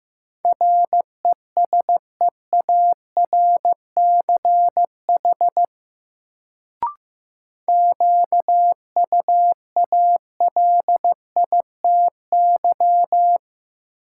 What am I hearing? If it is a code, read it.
Morse code, heard as a keyed tone, about 16 wpm, 700 Hz. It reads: RESEARCH QUALITY
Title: Sentences from Top 1000 Words - Review Part 4 - New Word Drill - 15wpm